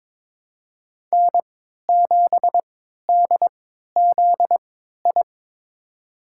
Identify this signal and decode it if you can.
Morse code, heard as a keyed tone, about 23 wpm, spaced as for 13 wpm, 700 Hz. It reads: N7DZI
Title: USA Call Signs - 5 Characters - 3x Character Spacing - 22wpm